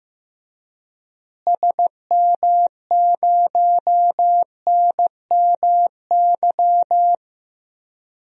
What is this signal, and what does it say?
Morse code, heard as a keyed tone, about 15 wpm, 700 Hz. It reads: SM0NMY
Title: DXCC-100 Call Signs - 6 Characters - Repeated 2x - 15wpm